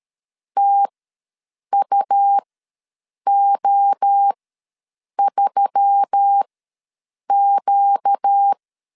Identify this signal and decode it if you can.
Morse code, heard as a keyed tone, about 13 wpm, spaced as for 7 wpm, 800 Hz. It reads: TUO3Q
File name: LEZIONI / 5 LBVC / 5 LBV ws 12wpm 3015